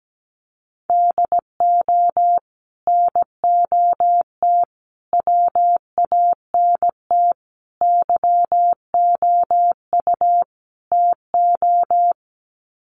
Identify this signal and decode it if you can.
Morse code, heard as a keyed tone, about 17 wpm, 700 Hz. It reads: DO NOT WANT YOU TO
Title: Sets of 5 Words 17wpm